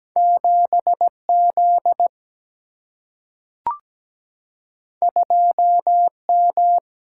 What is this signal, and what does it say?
Morse code, heard as a keyed tone, about 17 wpm, 700 Hz. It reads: 7 Z 2 M
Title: Single-Letter Single-Number 17wpm